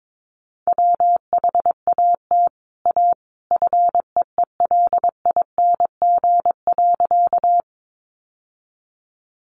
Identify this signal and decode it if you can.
Morse code, heard as a keyed tone, about 22 wpm, 700 Hz. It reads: WHAT A FEELING.
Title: Sentences from Top 1200 Words 22wpm